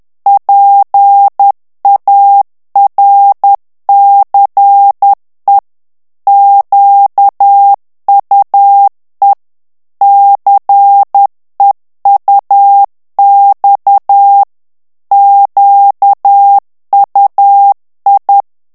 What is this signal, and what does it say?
Morse code, heard as a keyed tone, about 10 wpm, 800 Hz. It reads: PARCE QUE CEUX QUI
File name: FRANCAIS TEXTE CLAIR 2 VITESSE 10 MOTS MINUTES